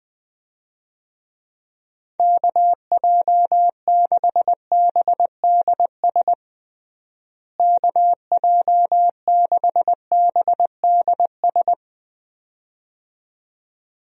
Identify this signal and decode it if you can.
Morse code, heard as a keyed tone, about 20 wpm, 700 Hz. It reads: KJ6BDS KJ6BDS
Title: USA Call Signs - 6 Characters - Repeated 2x - 20wpm